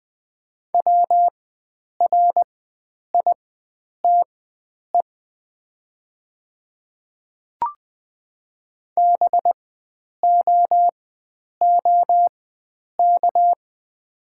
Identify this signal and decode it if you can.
Morse code, heard as a keyed tone, about 20 wpm, spaced as for 9 wpm, 700 Hz. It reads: WRITE BOOK
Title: 4x Character Spacing - Top 300 Words - 20wpm